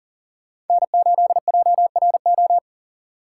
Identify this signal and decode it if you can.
Morse code, heard as a keyed tone, about 52 wpm, 700 Hz. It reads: N8JRO